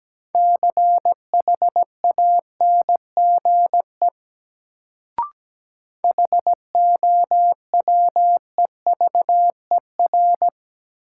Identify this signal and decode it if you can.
Morse code, heard as a keyed tone, about 17 wpm, 700 Hz. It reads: CHANGE HOWEVER
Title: Sentences from Top 600 Words - New Word Drill - 17wpm